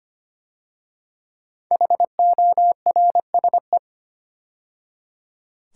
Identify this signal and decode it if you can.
Morse code, heard as a keyed tone, about 25 wpm, 700 Hz. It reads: HORSE
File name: 5 Letter Words - First 3 Letters Given - 25wpm